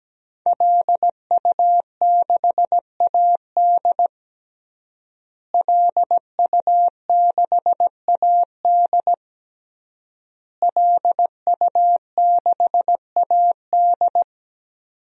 Morse code, heard as a keyed tone, about 17 wpm, 700 Hz. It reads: LU6AD LU6AD LU6AD
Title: DXCC-100 Call Signs - 5 Characters - Repeated 3x - 17wpm